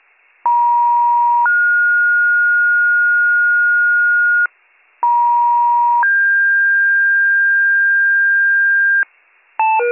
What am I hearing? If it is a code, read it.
Two-tone page, heard as a one-second tone, then a three-second tone, about 950 then 1430 Hz, about 950 then 1600 Hz, with noise. It is heard from 0.5 to 9.0 s.